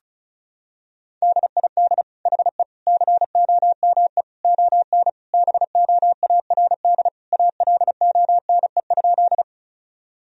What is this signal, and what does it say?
Morse code, heard as a keyed tone, about 34 wpm, 700 Hz. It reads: DID HE COME ON BOARD ALONE?
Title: Sentences from Top 1000 Words - Review Part 4 - 35wpm